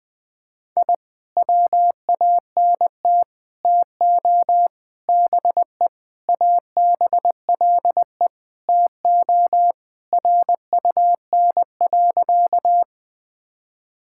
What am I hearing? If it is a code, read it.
Morse code, heard as a keyed tone, about 20 wpm, 700 Hz. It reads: I WANT TO BE ABLE TO RUN.
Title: Sentences with 3-Word Phrases - 20wpm